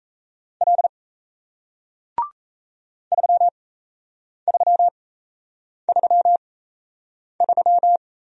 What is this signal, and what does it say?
Morse code, heard as a keyed tone, about 38 wpm, spaced as for 8 wpm, 700 Hz. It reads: L 3333